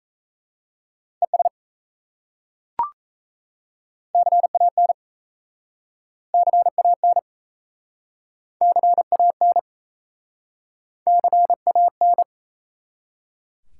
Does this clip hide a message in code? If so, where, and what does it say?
Yes: Morse code from 1.2 to 12.2 s, heard as a keyed tone, about 37 wpm, 700 Hz. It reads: ES CAN CAN CAN CAN